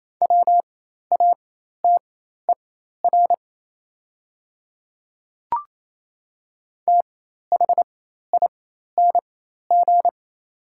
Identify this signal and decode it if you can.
Morse code, heard as a keyed tone, about 28 wpm, spaced as for 13 wpm, 700 Hz. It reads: WATER THING
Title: 4x Character Spacing - Top 100 Words - 28wpm